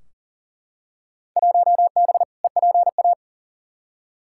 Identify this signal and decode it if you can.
Morse code, heard as a keyed tone, about 38 wpm, 700 Hz. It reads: 1B EPA
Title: Field Day Exchanges Encore - 40wpm